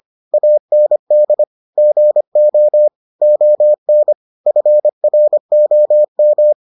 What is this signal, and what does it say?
Morse code, heard as a keyed tone, about 25 wpm, 600 Hz. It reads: AND GO ON FROM